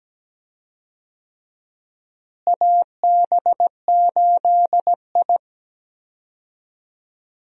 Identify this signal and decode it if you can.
Morse code, heard as a keyed tone, about 17 wpm, 700 Hz. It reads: AB8I
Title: USA Call Signs - 4 Characters - Repeated 2x - 17wpm